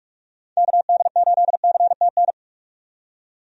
Morse code, heard as a keyed tone, about 46 wpm, 700 Hz. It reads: KD8CTN
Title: USA Call Signs - 6 Characters - 45wpm